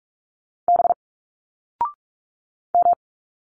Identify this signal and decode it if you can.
Morse code, heard as a keyed tone, about 46 wpm, 700 Hz. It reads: B M
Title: Single Letters 45wpm